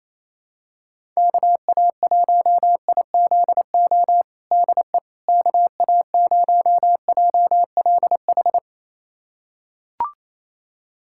Morse code, heard as a keyed tone, about 27 wpm, 700 Hz. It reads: KA1IZO DE KA0JLH
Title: USA Callsign DE Callsign - 6 Characters - 28wpm